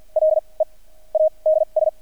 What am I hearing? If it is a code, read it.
Morse code, heard as a keyed tone, about 50 wpm, spaced as for 25 wpm, 630 Hz. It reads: LEADS
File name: archived / CRSnet